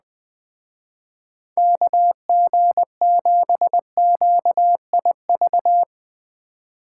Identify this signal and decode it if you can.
Morse code, heard as a keyed tone, about 20 wpm, 700 Hz. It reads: KG7QIV
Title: Speed Racing - USA 2x3 Call Signs 17wpm